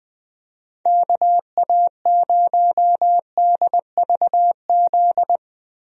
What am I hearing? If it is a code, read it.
Morse code, heard as a keyed tone, about 20 wpm, 700 Hz. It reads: KA0DVZ
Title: USA Call Signs - 6 Characters - 20wpm